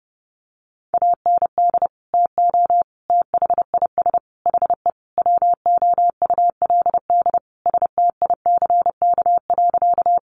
Morse code, heard as a keyed tone, about 28 wpm, 700 Hz. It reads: AND TO THIS HE WOULD STICK.
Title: Sentences from Top 1200 Words 30wpm